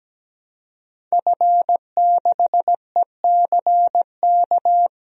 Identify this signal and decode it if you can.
Morse code, heard as a keyed tone, about 17 wpm, 700 Hz. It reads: F6ECK
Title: DXCC-100 Call Signs - 5 Characters - 17wpm